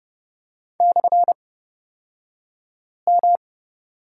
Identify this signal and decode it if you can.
Morse code, heard as a keyed tone, about 30 wpm, spaced as for 5 wpm, 700 Hz. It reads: /M
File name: Lesson 38d - Letter q - Two Characters 30wpm